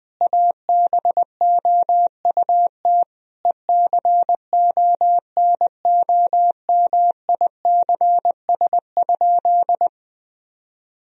Morse code, heard as a keyed tone, about 20 wpm, 700 Hz. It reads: ABOUT ECONOMICS?
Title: Sentences from Top 1000 Words - Review Part 3 - 20wpm